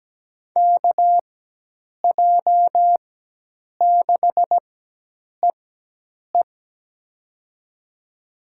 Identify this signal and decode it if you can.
Morse code, heard as a keyed tone, about 17 wpm, spaced as for 8 wpm, 700 Hz. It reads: KJ6EE
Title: USA Call Signs - 5 Characters - 4x Character Spacing - 17wpm